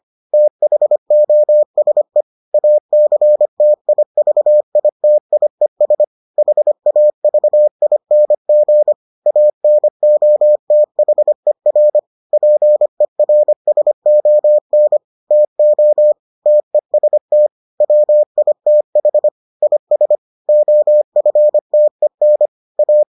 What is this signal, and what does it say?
Morse code, heard as a keyed tone, about 25 wpm, 600 Hz. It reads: THOSE ACTIVITIES HAVING ANOTHER PERSON TO TEST WITH IS OFTEN A